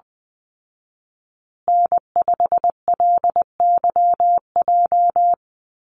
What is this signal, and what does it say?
Morse code, heard as a keyed tone, about 20 wpm, 700 Hz. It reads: N5LYJ
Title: Call Signs Repeated 2x - 20wpm